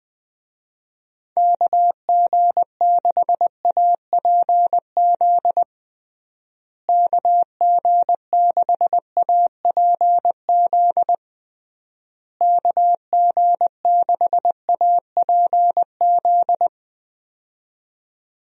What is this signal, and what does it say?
Morse code, heard as a keyed tone, about 20 wpm, 700 Hz. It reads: KG6APZ KG6APZ KG6APZ
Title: USA Call Signs - 6 Characters - Repeated 3x - 20wpm